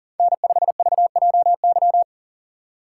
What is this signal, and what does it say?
Morse code, heard as a keyed tone, about 40 wpm, 700 Hz. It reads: N5VJY
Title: USA Call Signs - 5 Characters - 6x Character Spacing - 40wpm